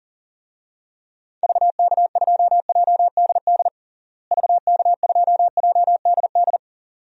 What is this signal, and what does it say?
Morse code, heard as a keyed tone, about 40 wpm, 700 Hz. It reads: VK2JDD VK2JDD
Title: DXCC-100 Call Signs - 6 Characters - Repeated 2x - 40wpm